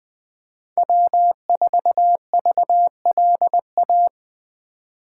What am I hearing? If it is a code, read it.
Morse code, heard as a keyed tone, about 20 wpm, 700 Hz. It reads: W4VLA
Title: USA Call Signs - 5 Characters - 4x Character Spacing - 20wpm